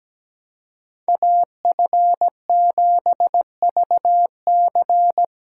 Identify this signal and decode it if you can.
Morse code, heard as a keyed tone, about 17 wpm, 700 Hz. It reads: AF7VC
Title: USA Callsign DE Callsign - 5 Characters - 17wpm